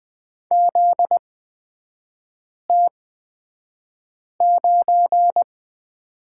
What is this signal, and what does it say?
Morse code, heard as a keyed tone, about 20 wpm, spaced as for 5 wpm, 700 Hz. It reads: ZT9
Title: 243 Lesson 39e - Letter z - Three Characters 20wpm